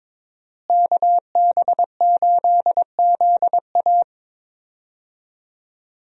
Morse code, heard as a keyed tone, about 22 wpm, 700 Hz. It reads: KB8ZA
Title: USA Call Signs - 5 Characters - Repeated 3x - 22wpm